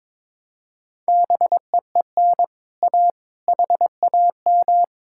Morse code, heard as a keyed tone, about 22 wpm, 700 Hz. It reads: BEEN A HAM
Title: Speed-Racing Common QSO Phrases - 22wpm